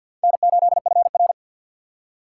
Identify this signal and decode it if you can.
Morse code, heard as a keyed tone, about 53 wpm, 700 Hz. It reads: N8FR
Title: USA Call Signs - 4 Characters - 2x Character Spacing - 50wpm